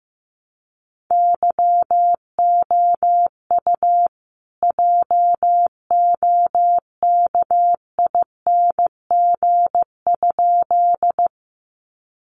Morse code, heard as a keyed tone, about 15 wpm, 700 Hz. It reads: YOU JOKING?
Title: A Sound Of Thunder 15wpm